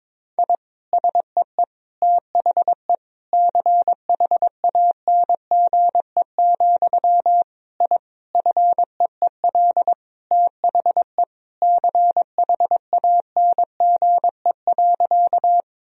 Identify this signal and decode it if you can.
Morse code, heard as a keyed tone, about 22 wpm, 700 Hz. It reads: I SEE THE CHANGE, I FEEL THE CHANGE.
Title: Sentences from Top 1000 Words - Review Part 1 - 22wpm